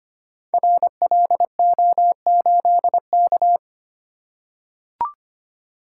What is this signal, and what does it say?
Morse code, heard as a keyed tone, about 25 wpm, 700 Hz. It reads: RLO8K
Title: Code Groups - 5 Letter:Numbers - 25wpm